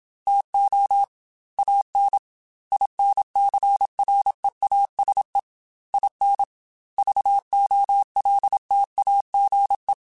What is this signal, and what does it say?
Morse code, heard as a keyed tone, about 27 wpm, 800 Hz. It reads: TO AN INCREASE IN VOLTAGE
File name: Technical Subjects / Technical Subjects-mp3s / XC @25